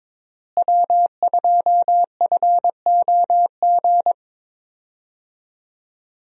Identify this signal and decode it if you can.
Morse code, heard as a keyed tone, about 22 wpm, 700 Hz. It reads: W2FOG
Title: USA Call Signs - 5 Characters - Repeated 3x - 22wpm